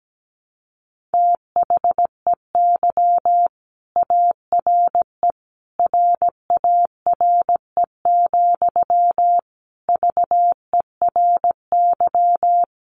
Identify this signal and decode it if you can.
Morse code, heard as a keyed tone, about 17 wpm, 700 Hz. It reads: THEY ARE RARE, VERY
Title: Sentences from Top 1100 Words 17wpm